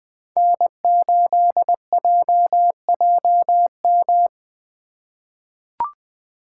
Morse code, heard as a keyed tone, about 20 wpm, 700 Hz. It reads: N8JJM
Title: 2023 POTA Hunter Call Signs - 20wpm